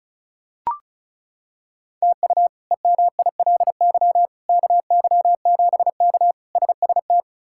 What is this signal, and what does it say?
Morse code, heard as a keyed tone, about 36 wpm, 700 Hz. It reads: TU EMILY KY7K SST
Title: K1USN SST QSO Practice - 35wpm